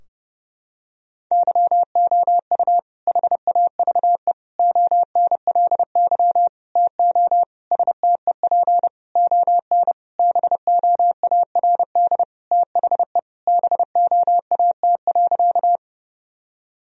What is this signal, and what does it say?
Morse code, heard as a keyed tone, about 30 wpm, 700 Hz. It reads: YOU HAVE ONLY TO STEP ON BOARD THE BOAT.
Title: Sentences from Top 1000 Words - Review Part 4 - 30wpm